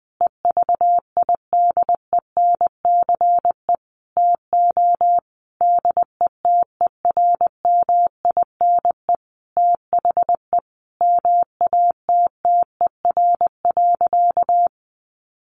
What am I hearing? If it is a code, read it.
Morse code, heard as a keyed tone, about 20 wpm, 700 Hz. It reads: EVIDENCE TO DETERMINE THE MATTER.
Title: Sentences from Top 1300 Words 20wpm